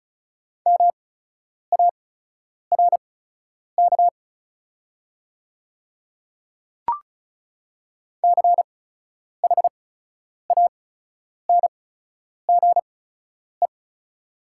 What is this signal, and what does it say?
Morse code, heard as a keyed tone, about 36 wpm, spaced as for 10 wpm, 690 Hz. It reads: MARK CHANGE
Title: 8x Character Spacing - Top 300 Words - 35wpm